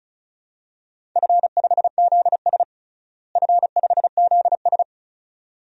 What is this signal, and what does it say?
Morse code, heard as a keyed tone, about 36 wpm, 700 Hz. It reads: F5ZS F5ZS
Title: DXCC-100 Call Signs - 4 Characters - Repeated 2x - 35wpm